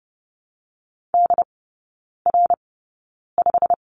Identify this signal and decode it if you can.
Morse code, heard as a keyed tone, about 30 wpm, spaced as for 9 wpm, 700 Hz. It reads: DR5
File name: course / 30:10wpm / 087 Lesson 15e - Number 5 - Three Characters 30wpm